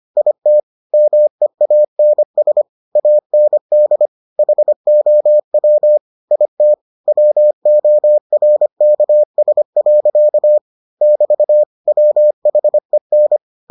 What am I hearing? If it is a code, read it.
Morse code, heard as a keyed tone, about 25 wpm, 600 Hz. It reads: IT MEANS AND HOW IT WORKS. <BT> WHEN